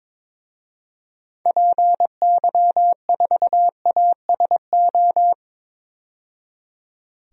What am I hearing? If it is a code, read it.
Morse code, heard as a keyed tone, about 22 wpm, 700 Hz. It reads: PY4ASO
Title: DXCC-100 Call Signs - 6 Characters - 22wpm